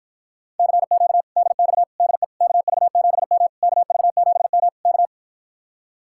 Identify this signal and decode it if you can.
Morse code, heard as a keyed tone, about 53 wpm, 700 Hz. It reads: CQ DX DE KV7M KV7M K